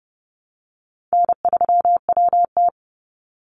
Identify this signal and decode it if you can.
Morse code, heard as a keyed tone, about 30 wpm, 700 Hz. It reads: N3WT
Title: Easy Call Signs Repeated 3x - 30wpm